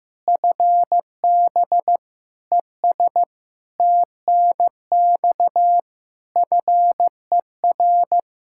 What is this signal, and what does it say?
Morse code, heard as a keyed tone, about 15 wpm, 700 Hz. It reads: FB ES TNX FER
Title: Practice QSO - Exchange 2 - Info - 15wpm